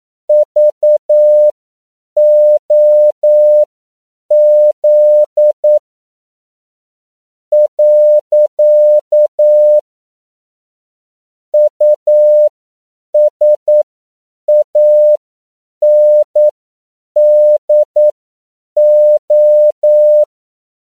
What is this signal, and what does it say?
Morse code, heard as a keyed tone, about 9 wpm, spaced as for 7 wpm, 600 Hz. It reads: VOZ . USANDO
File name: CW / 10PPM